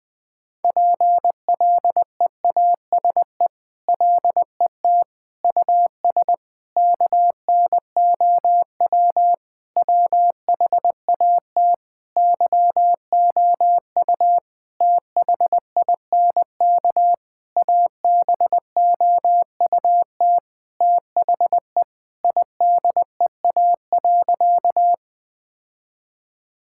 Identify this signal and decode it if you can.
Morse code, heard as a keyed tone, about 20 wpm, 700 Hz. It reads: PLEASE LET US KNOW WHAT YOU THINK ABOUT THE IDEA.
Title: Sentences with 6-Word Phrases - 20wpm